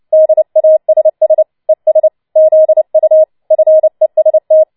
Morse code, heard as a keyed tone, about 29 wpm, 630 Hz. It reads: DASS ES ZU FEST
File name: Schneewittchen Teil3 30wpm